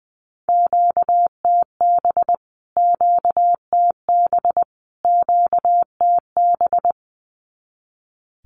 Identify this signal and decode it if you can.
Morse code, heard as a keyed tone, about 20 wpm, 700 Hz. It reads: QTB QTB QTB
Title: Q-Codes Repeated 3x 20wpm